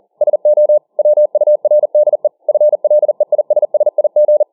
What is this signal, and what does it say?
Morse code, heard as a keyed tone, about 40 wpm, 600 Hz. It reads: SO WURDE FLEISSIG